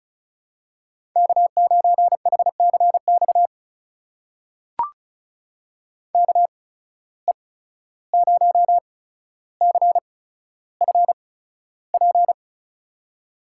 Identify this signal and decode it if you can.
Morse code, heard as a keyed tone, about 36 wpm, spaced as for 14 wpm, 700 Hz. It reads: K9HCX KE0CFP
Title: USA Call Signs - 8x Character Spacing - 35wpm